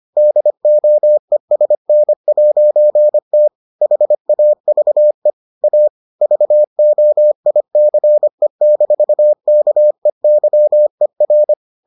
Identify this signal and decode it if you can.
Morse code, heard as a keyed tone, about 25 wpm, 600 Hz. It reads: DOESN'T HAVE A VOICE-KEYER